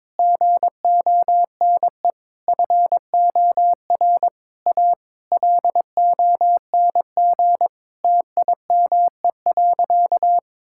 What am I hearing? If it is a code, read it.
Morse code, heard as a keyed tone, about 22 wpm, 700 Hz. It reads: GONE FOR A LONG TIME.
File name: practice / 22wpm / Sentences with 3-Word Phrases - 22wpm